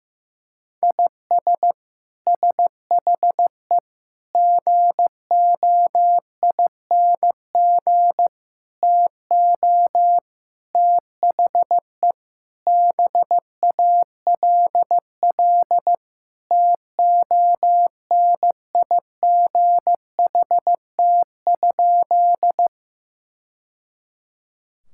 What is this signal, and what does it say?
Morse code, heard as a keyed tone, about 15 wpm, 700 Hz. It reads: IS SHE GOING TO THE BALL TONIGHT?
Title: Sentences from Top 1000 Words - Review Part 6 - 15wpm